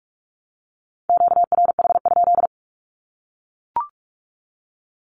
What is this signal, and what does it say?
Morse code, heard as a keyed tone, about 44 wpm, 700 Hz. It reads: QRH?
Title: Q-Codes 45wpm